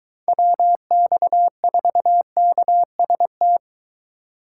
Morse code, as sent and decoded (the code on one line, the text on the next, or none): .-- -..- ....- -.- ... -
WX4KST